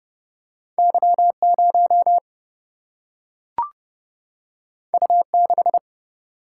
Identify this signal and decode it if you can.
Morse code, heard as a keyed tone, about 30 wpm, 700 Hz. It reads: Y 0 U 6
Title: Single-Letter Single-Number 30wpm